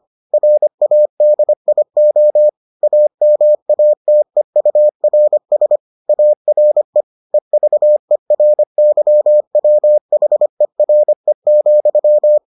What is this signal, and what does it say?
Morse code, heard as a keyed tone, about 25 wpm, 600 Hz. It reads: RADIO AMATEURS ARE EVERYWHERE,